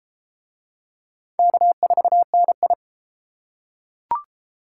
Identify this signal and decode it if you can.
Morse code, heard as a keyed tone, about 31 wpm, 700 Hz. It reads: K4NI